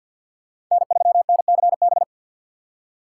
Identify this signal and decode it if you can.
Morse code, heard as a keyed tone, about 53 wpm, 700 Hz. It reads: N3NCB
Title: USA Call Signs - 5 Characters - 6x Character Spacing - 50wpm